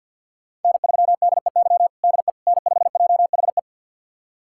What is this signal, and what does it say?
Morse code, heard as a keyed tone, about 50 wpm, 700 Hz. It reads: N3DEY DE N5JHE
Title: USA Callsign DE Callsign - 5 Characters - 50wpm